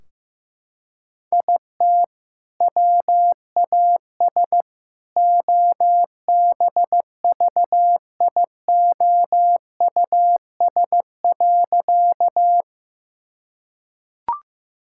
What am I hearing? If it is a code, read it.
Morse code, heard as a keyed tone, about 15 wpm, 700 Hz. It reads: IT WAS OBVIOUS.
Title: Sentences from Top 1600 Words 15wpm